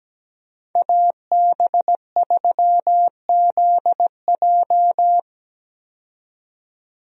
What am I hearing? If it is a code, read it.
Morse code, heard as a keyed tone, about 17 wpm, 700 Hz. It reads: AB3ZJ
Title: USA Call Signs - 5 Characters - 17wpm